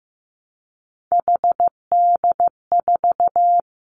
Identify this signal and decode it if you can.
Morse code, heard as a keyed tone, about 15 wpm, 700 Hz. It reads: HD4
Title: Three Challenging Letter-Number Combinations 15wpm